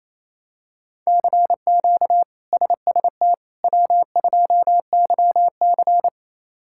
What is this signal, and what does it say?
Morse code, heard as a keyed tone, about 28 wpm, 700 Hz. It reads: CQ SST W2YC
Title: K1USN SST QSO Practice - 28wpm